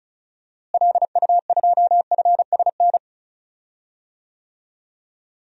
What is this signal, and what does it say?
Morse code, heard as a keyed tone, about 36 wpm, 700 Hz. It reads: LU2FSN
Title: DXCC-100 Call Signs - 6 Characters - Repeated 3x - 35wpm